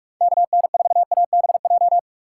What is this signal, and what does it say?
Morse code, heard as a keyed tone, about 45 wpm, 700 Hz. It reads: KN4ABJ
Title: USA Call Signs - 6 Characters - 2x Character Spacing - 45wpm